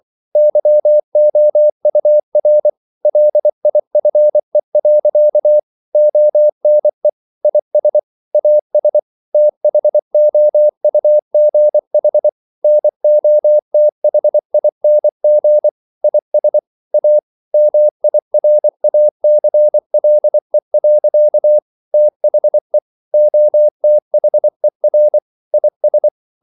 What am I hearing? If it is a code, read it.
Morse code, heard as a keyed tone, about 24 wpm, 600 Hz. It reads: YOUR LIFE. ONE IS AS THOUGH NOTHING IS A MIRACLE. THE OTHER IS
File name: CW / Practice / EnglishQuotations0424